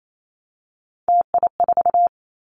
Morse code, heard as a keyed tone, about 28 wpm, 700 Hz. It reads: TI4
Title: Callsign Prefixes for Top 100 DXCC - 28wpm